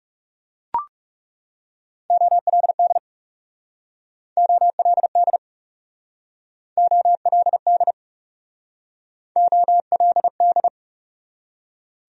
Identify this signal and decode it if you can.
Morse code, heard as a keyed tone, about 40 wpm, 700 Hz. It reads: OLD OLD OLD OLD